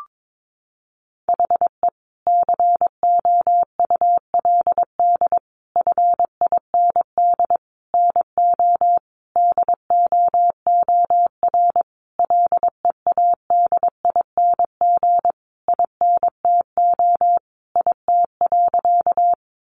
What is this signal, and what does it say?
Morse code, heard as a keyed tone, about 22 wpm, 700 Hz. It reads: HE COULD FIND NO DOOR LEADING INTO IT.
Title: Sentences from Top 1100 Words 22wpm